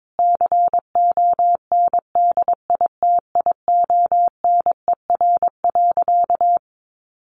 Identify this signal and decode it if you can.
Morse code, heard as a keyed tone, about 22 wpm, 700 Hz. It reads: CONDITIONER.
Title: Additional Easy Phrases 22wpm